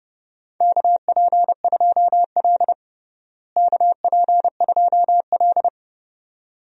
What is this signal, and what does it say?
Morse code, heard as a keyed tone, about 30 wpm, 700 Hz. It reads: KP2L KP2L
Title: DXCC-100 Call Signs - 4 Characters - Repeated 2x - 30wpm